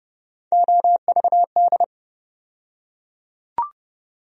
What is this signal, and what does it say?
Morse code, heard as a keyed tone, about 30 wpm, 700 Hz. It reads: OVD
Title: USA Call Sign Postfixes - 3 Letters 30wpm